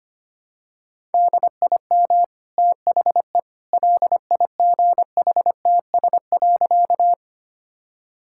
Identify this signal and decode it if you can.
Morse code, heard as a keyed tone, about 25 wpm, 700 Hz. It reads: DIM THE LIGHTS.
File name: Easy Sentences with Hints - 25wpm